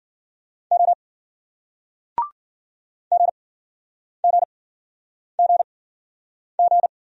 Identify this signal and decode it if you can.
Morse code, heard as a keyed tone, about 55 wpm, spaced as for 9 wpm, 700 Hz. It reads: O GGGG